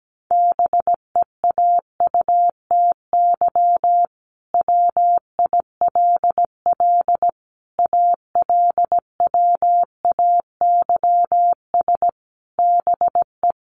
Morse code, heard as a keyed tone, about 17 wpm, 700 Hz. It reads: BEAUTY WILL ALWAYS BE
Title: Sentences from Top 1300 Words 17wpm